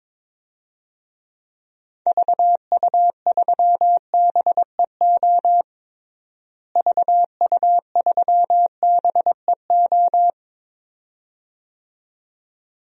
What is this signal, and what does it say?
Morse code, heard as a keyed tone, about 22 wpm, 700 Hz. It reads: VU3BEO VU3BEO
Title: DXCC-100 Call Signs - 6 Characters - Repeated 2x - 22wpm